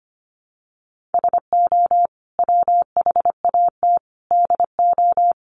Morse code, heard as a keyed tone, about 25 wpm, 700 Hz. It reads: SO WHAT DO
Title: Sets of 5 Words Encore2 25wpm